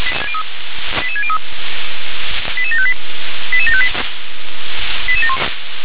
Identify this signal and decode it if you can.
Five-tone selective call (ZVEI), heard as quick runs of five tones, tones of about 70 ms. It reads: 88603 88758 88758